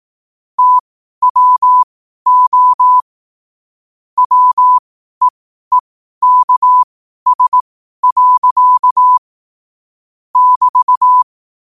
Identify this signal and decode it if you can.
Morse code, heard as a keyed tone, about 18 wpm, spaced as for 12 wpm, 1000 Hz. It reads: TWO WEEKS. <BT>
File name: 12wpm